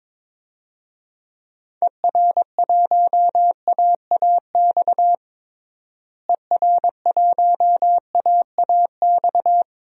Morse code, heard as a keyed tone, about 22 wpm, 700 Hz. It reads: ER1AAX ER1AAX
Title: DXCC-100 Call Signs - 6 Characters - Repeated 2x - 22wpm